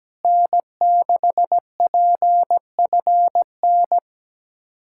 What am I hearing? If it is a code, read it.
Morse code, heard as a keyed tone, about 17 wpm, 700 Hz. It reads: N6PFN
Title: USA Callsign DE Callsign - 5 Characters - 17wpm